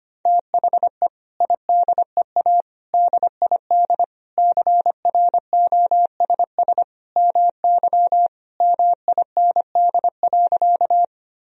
Morse code, heard as a keyed tone, about 25 wpm, 700 Hz. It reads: THE IDEA DID CROSS MY MIND.
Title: Sentences from Top 1000 Words - Review Part 1 - 25wpm